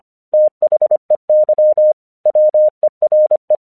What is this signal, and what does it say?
Morse code, heard as a keyed tone, about 24 wpm, 610 Hz. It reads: THEY WERE
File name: 1 English 25wpm / AQC20